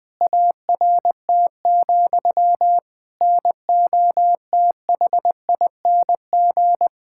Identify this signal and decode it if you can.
Morse code, heard as a keyed tone, about 21 wpm, 700 Hz. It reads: ART, NOTHING
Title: Sentences from Top 1000 Words - Review Part 6 - 20wpm